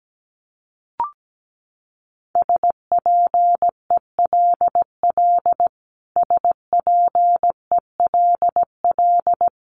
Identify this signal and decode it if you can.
Morse code, heard as a keyed tone, about 17 wpm, 700 Hz. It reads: SPELL SPELL
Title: Top 200 Words Repeated 17wpm